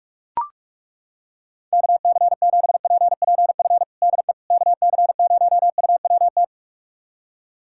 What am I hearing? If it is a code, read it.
Morse code, heard as a keyed tone, about 41 wpm, 700 Hz. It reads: KC7PPF DE KC0UWT
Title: USA Callsign DE Callsign - 6 Characters - 45wpm